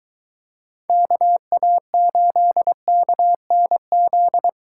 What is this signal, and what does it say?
Morse code, heard as a keyed tone, about 23 wpm, 700 Hz. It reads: KA8KNZ